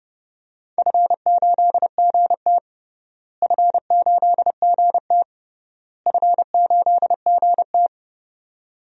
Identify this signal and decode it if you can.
Morse code, heard as a keyed tone, about 29 wpm, 700 Hz. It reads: F8GT F8GT F8GT
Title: DXCC-100 Call Signs - 4 Characters - Repeated 3x - 30wpm